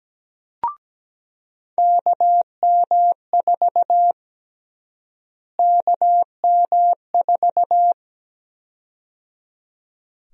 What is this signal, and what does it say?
Morse code, heard as a keyed tone, about 17 wpm, 700 Hz. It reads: KM4 KM4
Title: USA Call Sign Prefixes - Repeated 2x - 17wpm